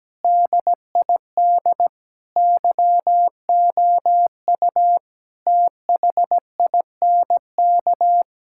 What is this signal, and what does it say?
Morse code, heard as a keyed tone, about 17 wpm, 700 Hz. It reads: DID YOU THINK
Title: Sentences from Top 1000 Words - Review Part 5 - 17wpm